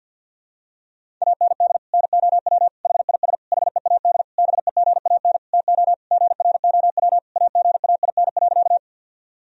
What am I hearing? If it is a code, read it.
Morse code, heard as a keyed tone, about 51 wpm, 700 Hz. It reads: AND NOW HIS HEAD BEGAN TO GROW AGAIN.
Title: Sentences from Top 1000 Words - Review Part 1 - 50wpm